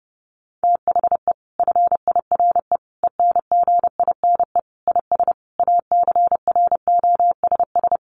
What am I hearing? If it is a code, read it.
Morse code, heard as a keyed tone, about 30 wpm, 700 Hz. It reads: THE FIRE ENGINE IS ACROSS
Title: Sentences from Top 1500 Words 30wpm